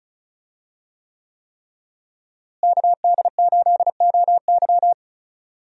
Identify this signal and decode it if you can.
Morse code, heard as a keyed tone, about 36 wpm, 700 Hz. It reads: KD8OY
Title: USA Call Signs - 5 Characters - Repeated 3x - 35wpm